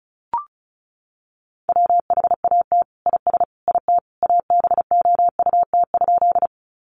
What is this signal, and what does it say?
Morse code, heard as a keyed tone, about 34 wpm, 700 Hz. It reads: WHAT IS IT ABOUT?
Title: Sentences from Top 100 Words 35wpm